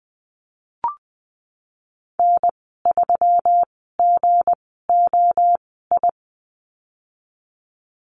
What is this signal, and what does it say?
Morse code, heard as a keyed tone, about 20 wpm, spaced as for 15 wpm, 700 Hz. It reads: N3GOI
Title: 186 Lesson 30g - Letter b - Call Signs 20wpm